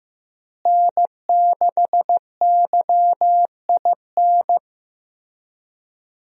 Morse code, heard as a keyed tone, about 15 wpm, 700 Hz. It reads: N6YIN
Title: USA Call Signs - 5 Characters - Repeated 2x - 15wpm